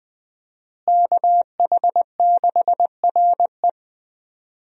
Morse code, heard as a keyed tone, about 20 wpm, 700 Hz. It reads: KH6RE
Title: USA Call Signs - 5 Characters - Repeated 2x - 20wpm